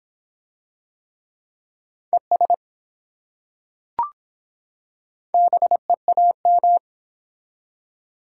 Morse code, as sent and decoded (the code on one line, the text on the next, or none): . ... / -... .- --
ES BEAM